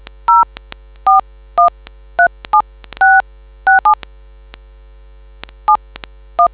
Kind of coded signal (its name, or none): DTMF